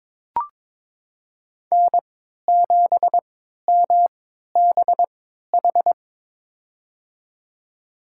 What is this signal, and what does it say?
Morse code, heard as a keyed tone, about 22 wpm, spaced as for 13 wpm, 700 Hz. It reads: N7MBH
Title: USA Call Signs - 3x Character Spacing - 22wpm